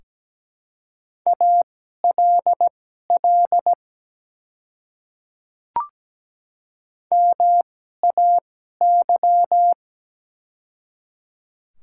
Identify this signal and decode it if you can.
Morse code, heard as a keyed tone, about 17 wpm, spaced as for 12 wpm, 690 Hz. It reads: ALL MAY
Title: 2x Character Spacing - Top 300 Words - 17wpm